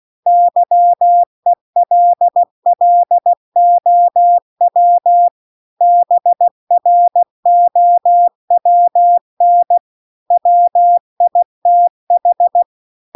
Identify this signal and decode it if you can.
Morse code, heard as a keyed tone, about 16 wpm, 700 Hz. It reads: YELLOW BROWN WITH
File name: CW-The-Big-Twoheaded-River-2-16WPM